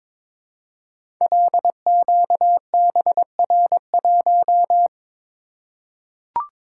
Morse code, heard as a keyed tone, about 22 wpm, 700 Hz. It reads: LQBR1
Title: Code Groups - 5 Letter:Numbers - 22wpm